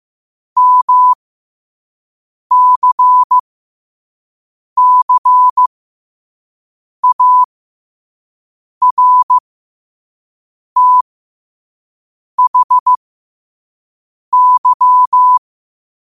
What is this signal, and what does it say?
Morse code, heard as a keyed tone, about 15 wpm, spaced as for 5 wpm, 1000 Hz. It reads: MCCARTHY
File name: 5wpm